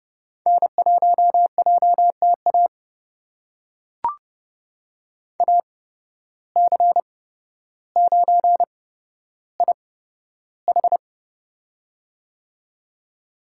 Morse code, heard as a keyed tone, about 30 wpm, spaced as for 13 wpm, 700 Hz. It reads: N1JTA AC9IH
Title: USA Call Signs - 5 Characters - 8x Character Spacing - 30wpm